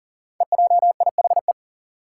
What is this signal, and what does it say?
Morse code, heard as a keyed tone, about 42 wpm, 700 Hz. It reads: EJIHE
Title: Five Challenging Letter-Number Combinations 40wpm